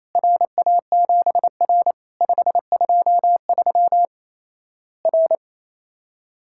Morse code, heard as a keyed tone, about 28 wpm, 700 Hz. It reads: RA7R 523 R
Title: Worked All Europe DX Contest - QTC Exchange - 28wpm